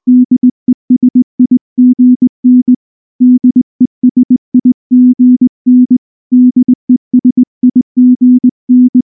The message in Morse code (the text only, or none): DESIGN DESIGN DESIGN